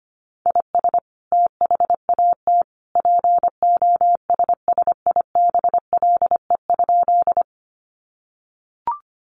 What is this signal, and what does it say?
Morse code, heard as a keyed tone, about 25 wpm, 700 Hz. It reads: IS THAT POSSIBLE?
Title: Sentences from Top 500 Words 25wpm